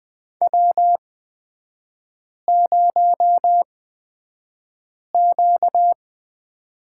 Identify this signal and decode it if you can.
Morse code, heard as a keyed tone, about 20 wpm, spaced as for 5 wpm, 700 Hz. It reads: W0Q
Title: 255 Lesson 41e - Period - Three Characters 20wpm